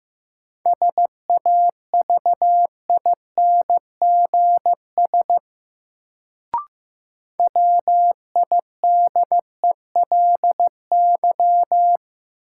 Morse code, heard as a keyed tone, about 15 wpm, 700 Hz. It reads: SAVINGS WIDELY
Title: Sentences from Top 1200 Words - New Word Drill Alt - 15wpm